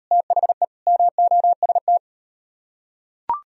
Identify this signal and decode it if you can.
Morse code, heard as a keyed tone, about 36 wpm, 700 Hz. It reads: THE MOST